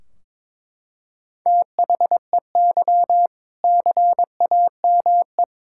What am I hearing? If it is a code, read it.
Morse code, heard as a keyed tone, about 22 wpm, 700 Hz. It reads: THEY CAME
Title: Sentences from Top 1600 Words 22wpm